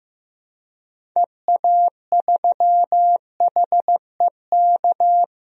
Morse code, heard as a keyed tone, about 15 wpm, 700 Hz. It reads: EA3HEK
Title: DXCC-100 Call Signs - 6 Characters - 15wpm